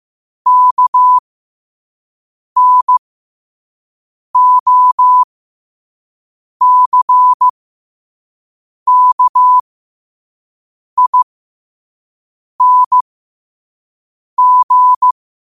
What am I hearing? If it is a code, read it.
Morse code, heard as a keyed tone, about 15 wpm, spaced as for 5 wpm, 1000 Hz. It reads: KNOCKING